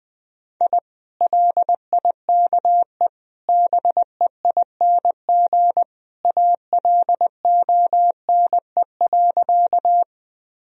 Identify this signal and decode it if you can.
Morse code, heard as a keyed tone, about 20 wpm, 700 Hz. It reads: I LIKE BEING ALONE.
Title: Sentences from Top 1000 Words - Review Part 3 - 20wpm